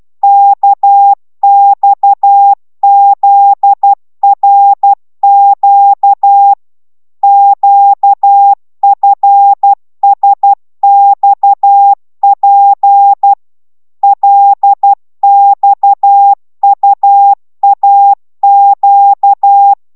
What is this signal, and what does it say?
Morse code, heard as a keyed tone, about 12 wpm, 800 Hz. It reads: KXZRQ QFSXP LXUAQ